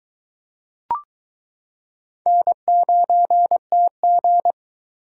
Morse code, heard as a keyed tone, about 22 wpm, 700 Hz. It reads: N9TG